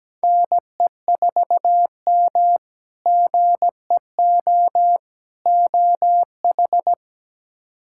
Morse code, heard as a keyed tone, about 17 wpm, 700 Hz. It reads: NE4M GEO OH
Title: K1USN SST QSO Practice - 17wpm